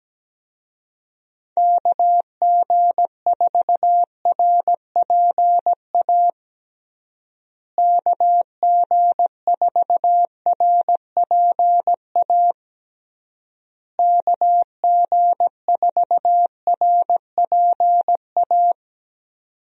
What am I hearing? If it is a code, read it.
Morse code, heard as a keyed tone, about 17 wpm, 700 Hz. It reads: KG4RPA KG4RPA KG4RPA
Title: USA Call Signs - 6 Characters - Repeated 3x - 17wpm